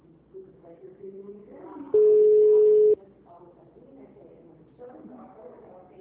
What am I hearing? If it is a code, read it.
Ringback tone (Europe, 425 Hz), with speech.